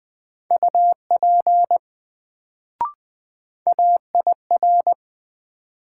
Morse code, heard as a keyed tone, about 21 wpm, 700 Hz. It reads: UP AIR
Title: Sets of 2 Words - New Word Drill - 20wpm